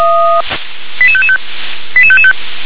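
Five-tone selective call (ZVEI), heard as quick runs of five tones, tones of about 70 ms. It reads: 88555 88555
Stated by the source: ZVEI five-tone call 88843 received on an SDR.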